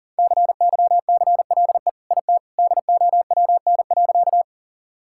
Morse code, heard as a keyed tone, about 40 wpm, 700 Hz. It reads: CYCLE IT DOWN.
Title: Sentences from Top 1000 Words - Review Part 6 - 40wpm